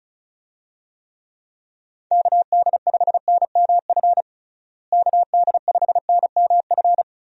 Morse code, heard as a keyed tone, about 36 wpm, 700 Hz. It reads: KD5NMF KD5NMF
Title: USA Call Signs - 6 Characters - Repeated 3x - 35wpm